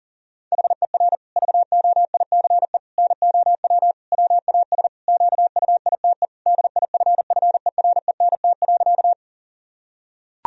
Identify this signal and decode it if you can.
Morse code, heard as a keyed tone, about 37 wpm, 700 Hz. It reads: HER VOICE NOW WAS QUITE DIFFERENT.
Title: Sentences from Top 1000 Words - Review Part 3 - 40wpm